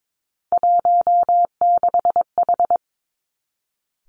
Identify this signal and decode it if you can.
Morse code, heard as a keyed tone, about 22 wpm, 700 Hz. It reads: 16H
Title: Three Challenging Letter-Number Combinations 22wpm